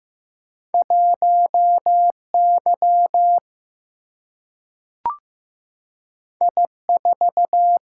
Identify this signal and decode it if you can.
Morse code, heard as a keyed tone, about 15 wpm, 700 Hz. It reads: 1Y I4
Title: Single-Letter Single-Number 15wpm